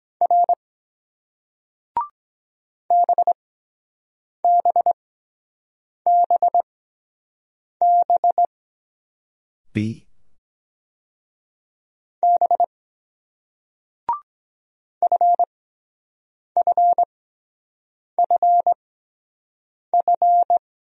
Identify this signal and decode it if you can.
Morse code, heard as a keyed tone, about 24 wpm, spaced as for 7 wpm, 700 Hz. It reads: R BBBB B FFFF